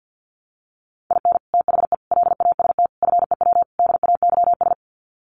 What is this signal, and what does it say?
Morse code, heard as a keyed tone, about 50 wpm, 700 Hz. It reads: IN THE LAST FEW DAYS
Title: Sets of 5 Words Encore2 50wpm